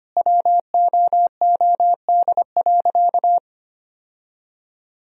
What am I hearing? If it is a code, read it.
Morse code, heard as a keyed tone, about 25 wpm, 700 Hz. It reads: WOOD.
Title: Sentences from Top 1000 Words - Review Part 3 - 25wpm